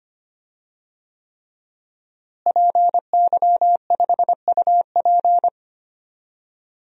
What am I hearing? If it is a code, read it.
Morse code, heard as a keyed tone, about 25 wpm, 700 Hz. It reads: PY5UP
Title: DXCC-100 Call Signs - 5 Characters - Repeated 3x - 25wpm